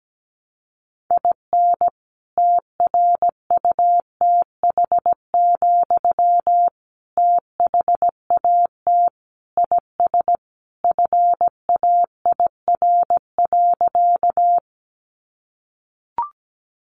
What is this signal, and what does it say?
Morse code, heard as a keyed tone, about 17 wpm, 700 Hz. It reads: IN TRUTH, THAT IS FAIR.
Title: Sentences from Top 1300 Words 17wpm